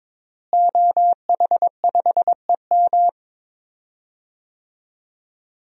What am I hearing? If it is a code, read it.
Morse code, heard as a keyed tone, about 22 wpm, 700 Hz. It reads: OH5EM
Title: DXCC-100 Call Signs - 5 Characters - Repeated 3x - 22wpm